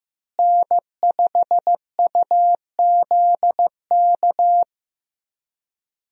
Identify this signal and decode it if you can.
Morse code, heard as a keyed tone, about 15 wpm, 700 Hz. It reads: N5UZK
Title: USA Call Signs - 5 Characters - Repeated 2x - 15wpm